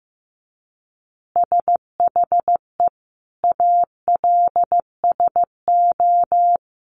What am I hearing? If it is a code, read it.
Morse code, heard as a keyed tone, about 15 wpm, 700 Hz. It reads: SHE ALSO
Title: Easy Phrases 15wpm